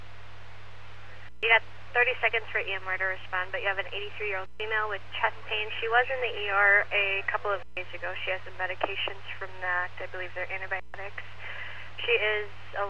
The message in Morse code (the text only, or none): none